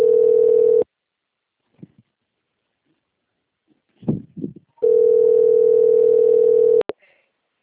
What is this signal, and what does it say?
Ringback tone (North America), 440 + 480 Hz.
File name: out- - - - - .26668